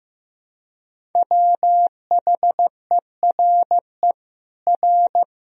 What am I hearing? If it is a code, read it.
Morse code, heard as a keyed tone, about 15 wpm, 700 Hz. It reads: WHERE R
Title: Sentences from Top 200 Words and Abbreviations 15wpm